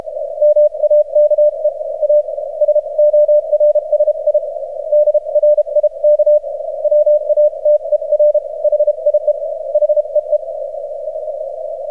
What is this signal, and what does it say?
Morse code, heard as a keyed tone, about 32 wpm, spaced as for 27 wpm, 600 Hz. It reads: MAKE AS ORSI DRIK WATER HIE HEE
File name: archived / CRSnet